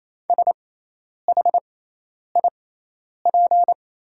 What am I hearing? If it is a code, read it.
Morse code, heard as a keyed tone, about 28 wpm, spaced as for 10 wpm, 700 Hz. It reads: SHIP
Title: 6x Character Spacing - Top 500 Words - 28wpm